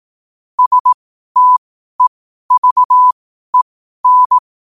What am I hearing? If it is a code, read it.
Morse code, heard as a keyed tone, about 18 wpm, spaced as for 12 wpm, 1000 Hz. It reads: STEVEN
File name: qotdfiles / 12wpm